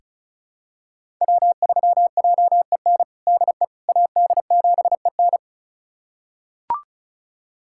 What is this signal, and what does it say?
Morse code, heard as a keyed tone, about 36 wpm, 700 Hz. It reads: W3JEN DE AD7EN
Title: USA Callsign DE Callsign - 5 Characters - 35wpm